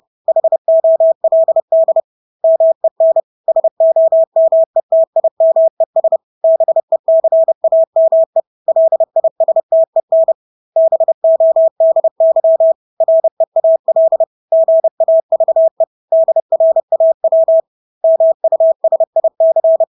Morse code, heard as a keyed tone, about 30 wpm, 650 Hz. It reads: HOLD MEN SOMETIMES BECAME LISTEN BODY REAL GAVE DRAW MUSIC
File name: Engelske ord 30-30 wpm